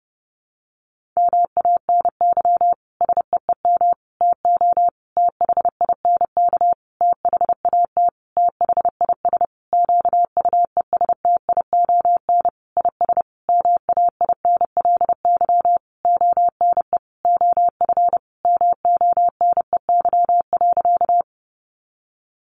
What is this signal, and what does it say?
Morse code, heard as a keyed tone, about 30 wpm, 700 Hz. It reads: MANY SEEM TO THINK THAT THIS QUESTION IS MAINLY ONE OF MONEY.
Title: Sentences from Top 1400 Words 30wpm